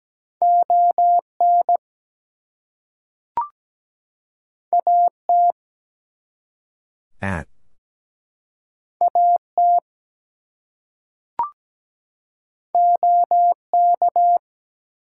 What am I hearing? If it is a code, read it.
Morse code, heard as a keyed tone, about 17 wpm, 700 Hz. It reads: ON AT AT OK
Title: Two Letter Words - 17wpm